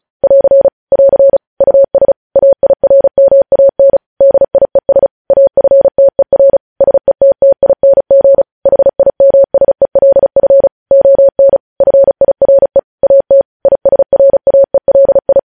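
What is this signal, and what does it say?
Morse code, heard as a keyed tone, about 36 wpm, 550 Hz. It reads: <AR> <AR> US AIRMAN DIES AFTER SETTING HIMSELF ON FIRE AT ISRAELI
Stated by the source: Morse code practice reading news headlines at 35 words per minute.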